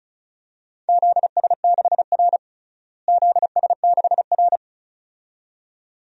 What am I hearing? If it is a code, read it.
Morse code, heard as a keyed tone, about 34 wpm, 700 Hz. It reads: ZS6R ZS6R
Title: DXCC-100 Call Signs - 4 Characters - Repeated 2x - 35wpm